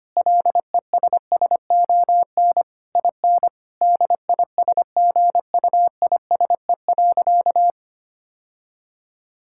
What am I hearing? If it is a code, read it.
Morse code, heard as a keyed tone, about 25 wpm, 700 Hz. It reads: LESSON IN DISGUISE.
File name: practice / 25wpm / Difficult Sentences with Hints - 25wpm